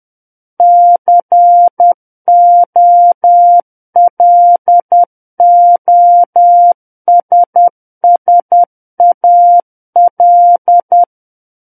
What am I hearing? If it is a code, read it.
Morse code, heard as a keyed tone, about 10 wpm, 700 Hz. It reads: COLOSSAL
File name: Book One / 01-Chapter 1 - 10 WPM